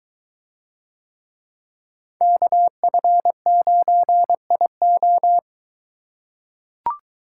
Morse code, heard as a keyed tone, about 22 wpm, 700 Hz. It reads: KF9IO